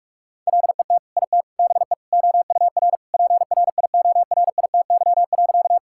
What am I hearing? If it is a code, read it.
Morse code, heard as a keyed tone, about 46 wpm, 700 Hz. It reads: LET IT BE OUR PRIORITY.
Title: Sentences from Top 1600 Words 45wpm